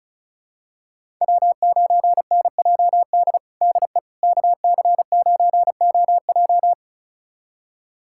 Morse code, heard as a keyed tone, about 34 wpm, 700 Hz. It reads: W9NJD DE KC9OJ
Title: USA Callsign DE Callsign - 5 Characters - 35wpm